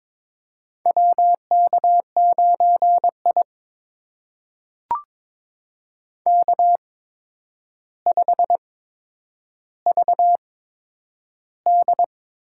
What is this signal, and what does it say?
Morse code, heard as a keyed tone, about 22 wpm, spaced as for 9 wpm, 700 Hz. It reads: WK9I K5VD
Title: USA Call Signs - 4 Characters - 8x Character Spacing - 22wpm